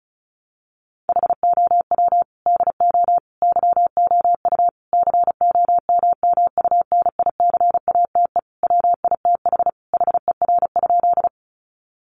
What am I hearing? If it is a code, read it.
Morse code, heard as a keyed tone, about 34 wpm, 700 Hz. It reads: HOW DO YOU COMMUNICATE WITH HER?
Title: Sentences from Top 1300 Words 35wpm